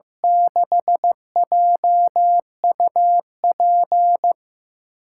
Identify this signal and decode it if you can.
Morse code, heard as a keyed tone, about 15 wpm, 700 Hz. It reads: 6JUP
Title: USA Call Signs - 6 Characters - 8x Character Spacing - 15wpm